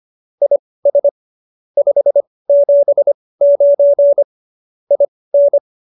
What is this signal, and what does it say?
Morse code, heard as a keyed tone, about 25 wpm, spaced as for 18 wpm, 570 Hz. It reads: IS 579 IN